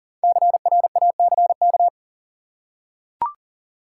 Morse code, heard as a keyed tone, about 40 wpm, 700 Hz. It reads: CRACK
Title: Five Letter Words - 40wpm